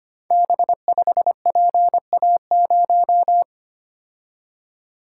Morse code, heard as a keyed tone, about 25 wpm, 700 Hz. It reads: B5PA0
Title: Code Groups - 5 Letter:Numbers - 25wpm